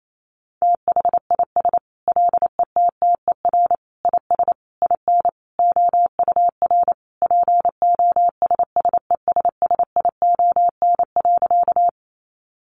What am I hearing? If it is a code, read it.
Morse code, heard as a keyed tone, about 28 wpm, 700 Hz. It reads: THIS LETTER IS IN OUR POSSESSION.
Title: Sentences from Top 1400 Words 28wpm